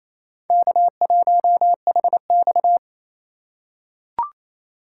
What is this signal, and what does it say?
Morse code, heard as a keyed tone, about 27 wpm, 700 Hz. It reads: K1HX
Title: USA Call Signs - 4 Characters - 2x Character Spacing - 28wpm